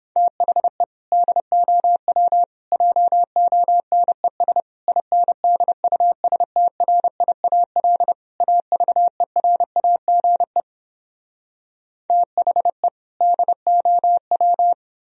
Morse code, heard as a keyed tone, about 29 wpm, 700 Hz. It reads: THE DOW JONES INDUSTRIAL AVERAGE THE DOW